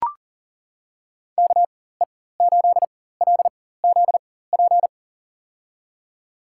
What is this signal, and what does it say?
Morse code, heard as a keyed tone, about 37 wpm, spaced as for 18 wpm, 700 Hz. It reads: KE8LZP
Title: USA Call Signs - 6 Characters - 4x Character Spacing - 40wpm